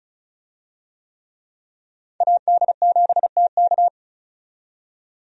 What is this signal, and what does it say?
Morse code, heard as a keyed tone, about 36 wpm, 700 Hz. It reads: AD7TK
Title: USA Call Signs - 5 Characters - 35wpm